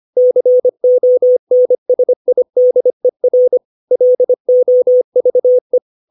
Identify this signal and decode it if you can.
Morse code, heard as a keyed tone, about 25 wpm, 500 Hz. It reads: CONSIDER LOVE
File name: 632 words / 632 Words 25WPM